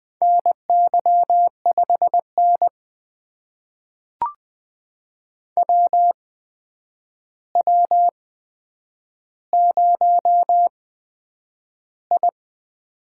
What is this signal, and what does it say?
Morse code, heard as a keyed tone, about 20 wpm, spaced as for 9 wpm, 700 Hz. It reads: NY5N WW0I
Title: USA Call Signs - 4 Characters - 8x Character Spacing - 20wpm